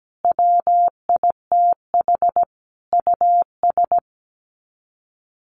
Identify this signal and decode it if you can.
Morse code, heard as a keyed tone, about 17 wpm, 700 Hz. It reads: WITH US
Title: Sets of 5 Words 17wpm